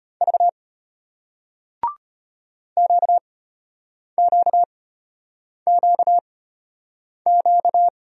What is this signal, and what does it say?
Morse code, heard as a keyed tone, about 35 wpm, spaced as for 8 wpm, 700 Hz. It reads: V QQQQ